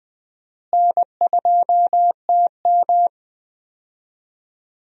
Morse code, heard as a keyed tone, about 20 wpm, 700 Hz. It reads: N2TM
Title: USA Call Signs - 4 Characters - Repeated 3x - 20wpm